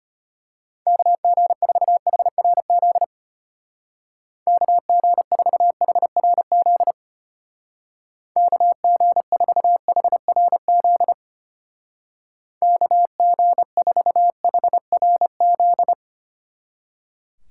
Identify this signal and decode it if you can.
Morse code, heard as a keyed tone, about 31 wpm, 700 Hz. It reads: KG4HRZ KG4HRZ KG4HRZ KG4HRZ
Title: Speed Racing - USA Call Signs 25wpm